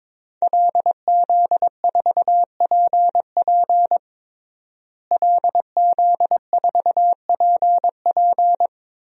Morse code, heard as a keyed tone, about 22 wpm, 700 Hz. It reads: LZ4PP LZ4PP
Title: DXCC-100 Call Signs - 5 Characters - Repeated 3x - 22wpm